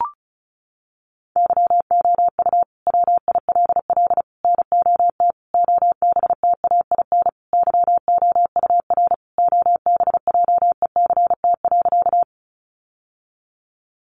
Morse code, heard as a keyed tone, about 34 wpm, 700 Hz. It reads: YOU WILL NOT OBTAIN YOUR OBJECT.
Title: Sentences from Top 1300 Words 35wpm